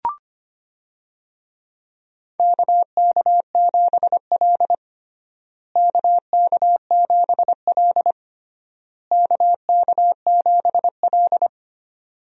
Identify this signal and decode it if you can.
Morse code, heard as a keyed tone, about 25 wpm, 700 Hz. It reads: KK7L KK7L KK7L
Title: USA Call Signs - 4 Characters - Repeated 3x - 25wpm